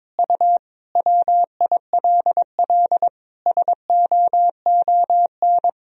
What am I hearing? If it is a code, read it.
Morse code, heard as a keyed tone, about 22 wpm, 700 Hz. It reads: U WILL SOON
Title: Sentences from Top 400 Words and Abbreviations 22wpm